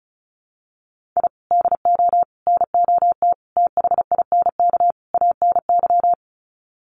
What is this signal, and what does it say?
Morse code, heard as a keyed tone, about 34 wpm, 700 Hz. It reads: I DO NOT THINK ANY
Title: Sets of 5 Words Encore 35wpm